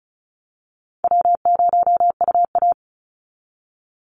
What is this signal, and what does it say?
Morse code, heard as a keyed tone, about 36 wpm, 700 Hz. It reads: W0UA
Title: Call Signs - 35wpm